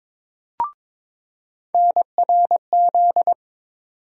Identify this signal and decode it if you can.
Morse code, heard as a keyed tone, about 21 wpm, 700 Hz. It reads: NRZ
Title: USA Call Sign Postfixes - 3 Letters 22wpm